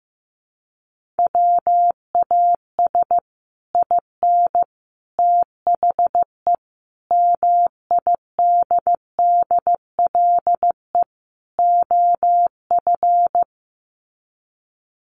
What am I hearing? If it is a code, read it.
Morse code, heard as a keyed tone, about 15 wpm, 700 Hz. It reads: WAS IN THE MIDDLE OF
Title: Sets of 5 Words Encore 15wpm